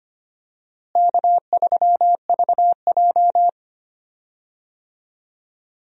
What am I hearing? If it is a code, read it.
Morse code, heard as a keyed tone, about 25 wpm, 700 Hz. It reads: K3VJ
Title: USA Call Signs - 4 Characters - Repeated 3x - 25wpm